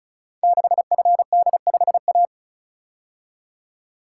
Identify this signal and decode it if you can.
Morse code, heard as a keyed tone, about 36 wpm, 700 Hz. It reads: 6FD5A
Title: Code Groups - 5 Letter:Numbers - 35wpm